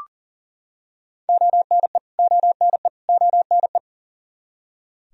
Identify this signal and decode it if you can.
Morse code, heard as a keyed tone, about 39 wpm, 700 Hz. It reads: ONE ONE ONE
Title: Top 100 Words Repeated 40wpm